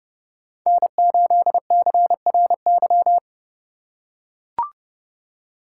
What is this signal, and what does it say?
Morse code, heard as a keyed tone, about 30 wpm, 700 Hz. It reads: N8CRY
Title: USA Call Signs - 5 Characters - 2x Character Spacing - 30wpm